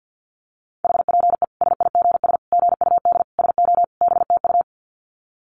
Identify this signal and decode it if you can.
Morse code, heard as a keyed tone, about 50 wpm, 700 Hz. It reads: HPE SIGS GUD SO BTU
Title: Common QSO Phrases 50wpm